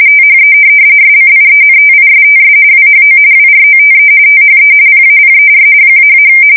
RTTY, about 45.45 baud. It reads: THE SCIENTISTS WHO WROTE THE PAPER --
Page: 170 Shift 2125/2295